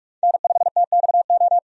Morse code, heard as a keyed tone, about 46 wpm, 700 Hz. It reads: N5TXO
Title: USA Call Signs - 5 Characters - 45wpm